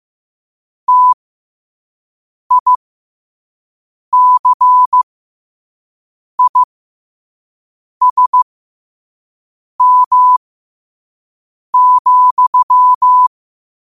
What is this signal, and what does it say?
Morse code, heard as a keyed tone, about 15 wpm, spaced as for 5 wpm, 1000 Hz. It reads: TICISM,
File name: qotdfiles / 5wpm